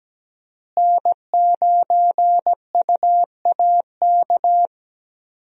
Morse code, heard as a keyed tone, about 17 wpm, 700 Hz. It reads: N9UAK
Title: USA Call Signs - 5 Characters - Repeated 2x - 17wpm